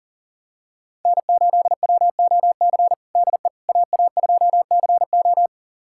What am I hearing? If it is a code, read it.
Morse code, heard as a keyed tone, about 40 wpm, 700 Hz. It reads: N8WOC DE AA2CO
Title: USA Callsign DE Callsign - 5 Characters - 40wpm